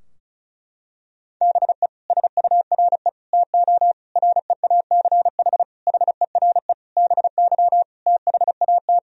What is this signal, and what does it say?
Morse code, heard as a keyed tone, about 36 wpm, 700 Hz. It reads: BE SURE TO REACH HERE BY THAT
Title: Sentences from Top 1000 Words - Review Part 2 - 35wpm